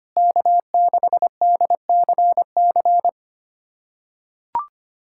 Morse code, heard as a keyed tone, about 25 wpm, 700 Hz. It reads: K6DCC
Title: USA Callsign DE Callsign - 5 Characters - 25wpm